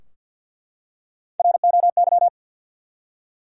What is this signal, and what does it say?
Morse code, heard as a keyed tone, about 52 wpm, 700 Hz. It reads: ROY
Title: USA Common Names - Max 3 Letters - 50wpm